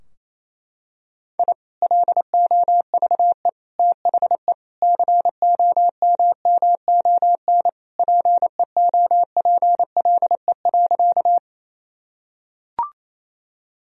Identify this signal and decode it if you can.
Morse code, heard as a keyed tone, about 28 wpm, 700 Hz. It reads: I LOVE THE COMMON PEOPLE.
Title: Sentences from Top 1000 Words - Review Part 3 - 28wpm